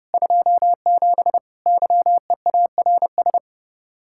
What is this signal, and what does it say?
Morse code, heard as a keyed tone, about 32 wpm, 700 Hz. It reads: 27 YEARS